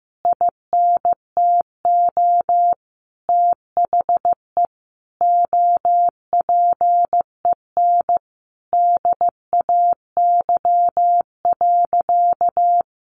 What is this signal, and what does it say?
Morse code, heard as a keyed tone, about 15 wpm, 700 Hz. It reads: INTO THE OPEN DAY.
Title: Sentences from Top 1500 Words 15wpm